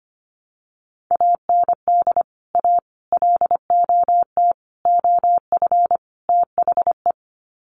Morse code, heard as a keyed tone, about 25 wpm, 700 Hz. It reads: AND A LOT OF THE
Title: Sets of 5 Words Encore 25wpm